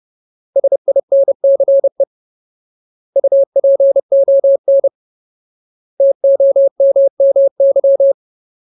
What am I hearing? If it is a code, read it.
Morse code, heard as a keyed tone, about 30 wpm, 550 Hz. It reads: SINCE UPON TOMMY
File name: Ultimate Word List / 30WPM Ultimate